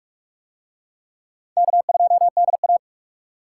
Morse code, heard as a keyed tone, about 41 wpm, 700 Hz. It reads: K2DA